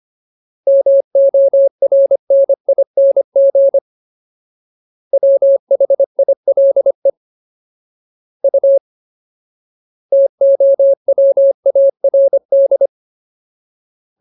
Morse code, heard as a keyed tone, about 25 wpm, 550 Hz. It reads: MORNING WHILE U TOWARD
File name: Ultimate Word List / 25WPM Ultimate